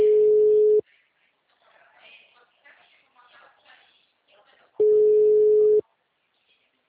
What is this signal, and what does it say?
Ringback tone (Europe, 425 Hz), with speech.